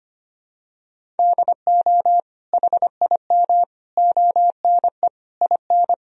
Morse code, heard as a keyed tone, about 25 wpm, 700 Hz. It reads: DO HIM ONE IN
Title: Sentences from Top 1000 Words - Review Part 1 - 25wpm